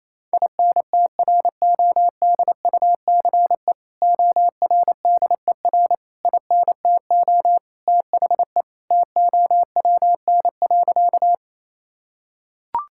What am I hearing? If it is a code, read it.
Morse code, heard as a keyed tone, about 28 wpm, 700 Hz. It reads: INTRODUCE ORDER INTO THE TOWN.
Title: Sentences from Top 1600 Words 28wpm